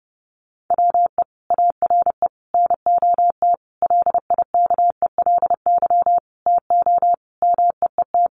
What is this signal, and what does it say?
Morse code, heard as a keyed tone, about 30 wpm, 700 Hz. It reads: WE ARE NOT LIKELY TO MEET
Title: Sentences from Top 1100 Words 30wpm